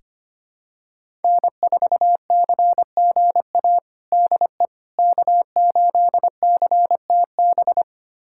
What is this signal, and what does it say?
Morse code, heard as a keyed tone, about 25 wpm, 700 Hz. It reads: N4CGA DE K8CTB
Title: USA Callsign DE Callsign - 5 Characters - 25wpm